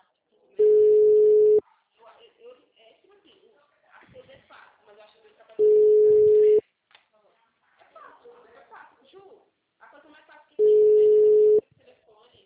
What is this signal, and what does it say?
Ringback tone (Europe, 425 Hz), with speech.
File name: out- -2002- - - .41019